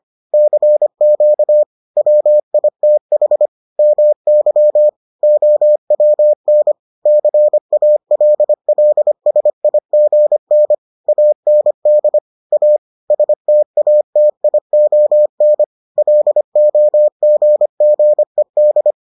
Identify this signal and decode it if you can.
Morse code, heard as a keyed tone, about 25 wpm, 600 Hz. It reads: CQ WITH MY OWN CALLSIGN AND A STATION LOGGED